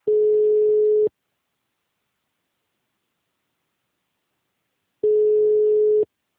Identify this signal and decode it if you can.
Ringback tone (Europe, 425 Hz).